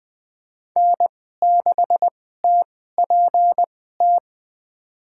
Morse code, heard as a keyed tone, about 20 wpm, spaced as for 14 wpm, 700 Hz. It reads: N6TPT
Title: USA Call Signs - 5 Characters - 2x Character Spacing - 20wpm